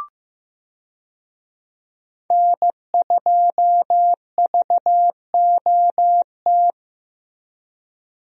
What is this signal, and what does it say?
Morse code, heard as a keyed tone, about 15 wpm, 700 Hz. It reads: N2VOT
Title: USA Call Signs - 5 Characters - Repeated 3x - 15wpm